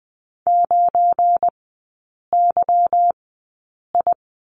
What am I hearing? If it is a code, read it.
Morse code, heard as a keyed tone, about 20 wpm, spaced as for 8 wpm, 690 Hz. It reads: 9YI
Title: 191 Lesson 31e - Letter v - Three Characters 20wpm